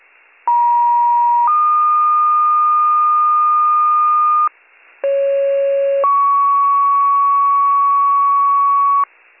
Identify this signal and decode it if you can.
Two-tone page, heard as a one-second tone, then a three-second tone, about 950 then 1220 Hz, about 550 then 1120 Hz. It is heard from 0.5 to 9.0 s.